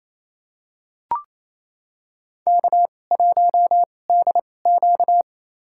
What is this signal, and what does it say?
Morse code, heard as a keyed tone, about 28 wpm, spaced as for 20 wpm, 700 Hz. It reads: K1DQ
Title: USA Call Signs - 2x Character Spacing - 28wpm